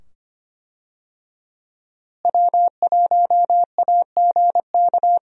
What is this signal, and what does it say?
Morse code, heard as a keyed tone, about 25 wpm, 700 Hz. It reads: W1AGK
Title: USA Call Signs - 5 Characters - Repeated 2x - 25wpm